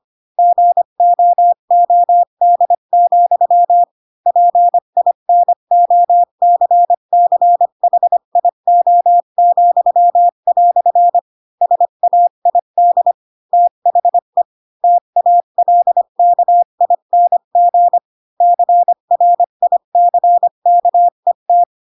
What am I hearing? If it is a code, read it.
Morse code, heard as a keyed tone, about 25 wpm, 700 Hz. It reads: GOOD, PINOCCHIO," SAID THE TALKING CRICKET
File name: Pinocchio / Pinocchio ch4 20 25 ik1hge